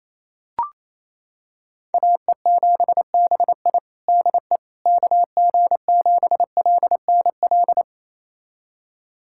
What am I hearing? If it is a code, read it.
Morse code, heard as a keyed tone, about 28 wpm, 700 Hz. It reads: AE7BI DE KG7LNL
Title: USA Callsign DE Callsign 28wpm